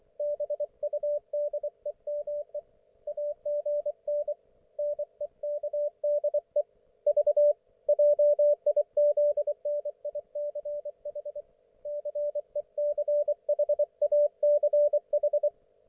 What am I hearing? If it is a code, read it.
Morse code, heard as a keyed tone, about 24 wpm, 590 Hz, with radio noise. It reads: BUDEME AGN NEKDE V JIZNICH CECHACH
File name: Archiv / CUC212 / ok1fhi cul212